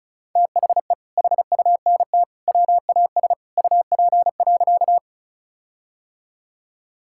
Morse code, heard as a keyed tone, about 36 wpm, 700 Hz. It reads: THE HUNT WAS UP.
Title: Sentences from Top 1600 Words 35wpm